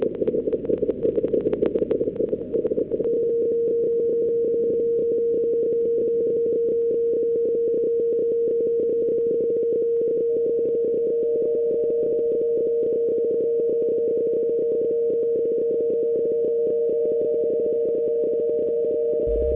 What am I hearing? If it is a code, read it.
PSK31, about 31.25 baud.
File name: luxorion / Radio / qso-bpsk31-g4ydo-s51ab- -1408z-14071-6khz